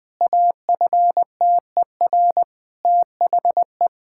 Morse code, heard as a keyed tone, about 20 wpm, 700 Hz. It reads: AFTER THE
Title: Sentences with 2-Word Phrases - 20wpm